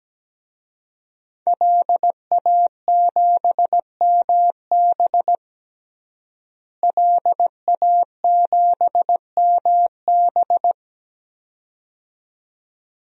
Morse code, heard as a keyed tone, about 17 wpm, 700 Hz. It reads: LA7MB LA7MB
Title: DXCC-100 Call Signs - 5 Characters - Repeated 2x - 17wpm